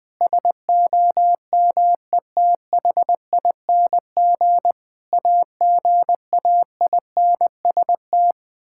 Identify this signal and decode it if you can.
Morse code, heard as a keyed tone, about 20 wpm, 700 Hz. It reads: SOMETHING AGAINST
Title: Sentences with 9-Word Phrases - 20wpm